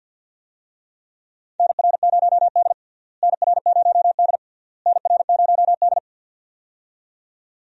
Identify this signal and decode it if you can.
Morse code, heard as a keyed tone, about 52 wpm, 700 Hz. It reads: NR0D NR0D NR0D
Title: USA Call Signs - 4 Characters - Repeated 3x - 50wpm